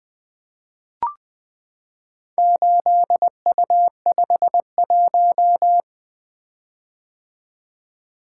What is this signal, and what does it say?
Morse code, heard as a keyed tone, about 20 wpm, 700 Hz. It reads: 8U51
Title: Four Letter-Number Combinations - 20wpm